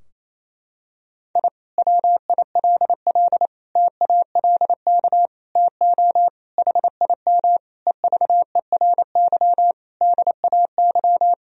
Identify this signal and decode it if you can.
Morse code, heard as a keyed tone, about 28 wpm, 700 Hz. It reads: I WILL TALK TO HIM EVERY DAY
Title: Sentences from Top 1000 Words - Review Part 2 - 28wpm